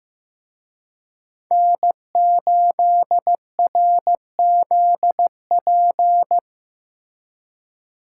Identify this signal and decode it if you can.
Morse code, heard as a keyed tone, about 18 wpm, 700 Hz. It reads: N8RZP
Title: Speed Racing - USA 1x3 Call Signs 15wpm